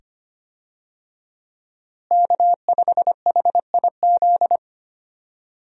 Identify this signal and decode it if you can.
Morse code, heard as a keyed tone, about 25 wpm, 700 Hz. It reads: K5HIZ
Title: USA Call Signs - 5 Characters - Repeated 2x - 25wpm